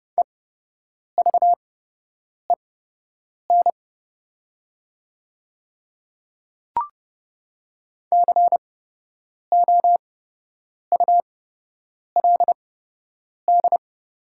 Morse code, heard as a keyed tone, about 30 wpm, spaced as for 8 wpm, 700 Hz. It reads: EVEN COULD
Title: 8x Character Spacing - Top 300 Words - 30wpm